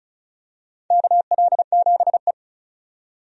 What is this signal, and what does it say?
Morse code, heard as a keyed tone, about 36 wpm, 690 Hz. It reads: KL7E
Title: DXCC-100 Call Signs - 4 Characters - 35wpm